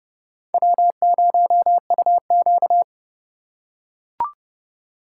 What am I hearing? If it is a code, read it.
Morse code, heard as a keyed tone, about 29 wpm, 700 Hz. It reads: W0UQ